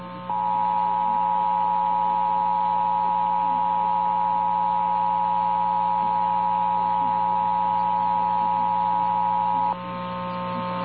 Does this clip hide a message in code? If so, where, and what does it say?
Yes: Emergency Alert System (SAME) tones (the 853 and 960 Hz attention tone) from 0.8 to 9.7 s, with noise.